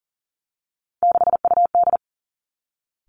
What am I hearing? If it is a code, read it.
Morse code, heard as a keyed tone, about 38 wpm, 700 Hz. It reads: 6UD
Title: Three Challenging Letter-Number Combinations 40wpm